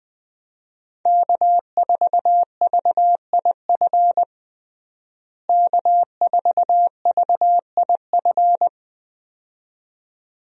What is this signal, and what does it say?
Morse code, heard as a keyed tone, about 21 wpm, 700 Hz. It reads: K4VIF K4VIF
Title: USA Call Signs - 5 Characters - Repeated 2x - 20wpm